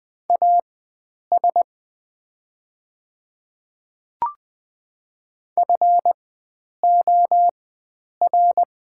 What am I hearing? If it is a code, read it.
Morse code, heard as a keyed tone, about 20 wpm, spaced as for 9 wpm, 690 Hz. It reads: AS FOR
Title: 4x Character Spacing - Top 100 Words - 20wpm